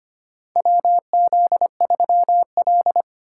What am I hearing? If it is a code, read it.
Morse code, heard as a keyed tone, about 25 wpm, 700 Hz. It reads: WZ3L
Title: USA Call Signs - 3x Character Spacing - 25wpm